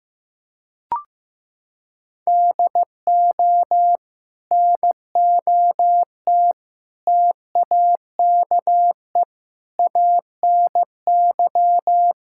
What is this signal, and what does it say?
Morse code, heard as a keyed tone, about 15 wpm, 700 Hz. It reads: DO NOT TAKE ANY
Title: Sentences from Top 1000 Words - Review Part 5 - 15wpm